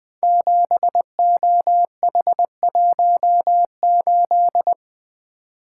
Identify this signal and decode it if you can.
Morse code, heard as a keyed tone, about 21 wpm, 700 Hz. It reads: 7OH18
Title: Code Groups - 5 Letter:Numbers - 20wpm